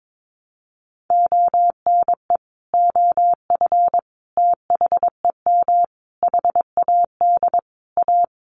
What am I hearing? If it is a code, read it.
Morse code, heard as a keyed tone, about 22 wpm, 700 Hz. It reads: ONE OF THEM HAD A
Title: Sentences from Top 1300 Words 22wpm